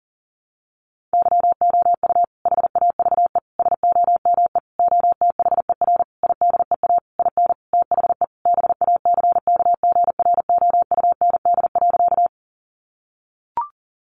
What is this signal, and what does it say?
Morse code, heard as a keyed tone, about 40 wpm, 700 Hz. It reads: YOU HAVE SOME OTHER IDEA IN THE BACKGROUND.
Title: Sentences from Top 1300 Words 40wpm